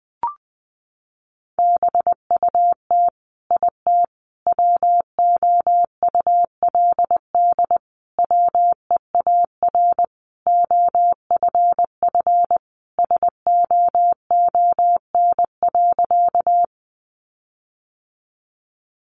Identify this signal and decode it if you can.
Morse code, heard as a keyed tone, about 20 wpm, 700 Hz. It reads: BUT IT WOULD WEAR OFF SOON.
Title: Sentences from Top 1500 Words 20wpm